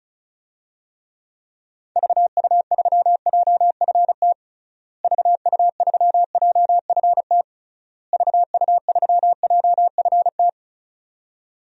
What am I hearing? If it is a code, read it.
Morse code, heard as a keyed tone, about 34 wpm, 700 Hz. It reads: VU3JFT VU3JFT VU3JFT
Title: DXCC-100 Call Signs - 6 Characters - Repeated 3x - 35wpm